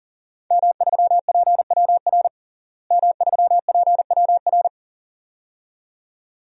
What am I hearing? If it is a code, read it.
Morse code, heard as a keyed tone, about 40 wpm, 700 Hz. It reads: M3PWR M3PWR
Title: DXCC-100 Call Signs - 5 Characters - Repeated 2x - 40wpm